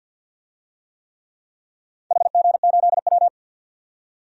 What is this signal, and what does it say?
Morse code, heard as a keyed tone, about 51 wpm, 700 Hz. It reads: HG8W
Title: DXCC-100 Call Signs - 4 Characters - 50wpm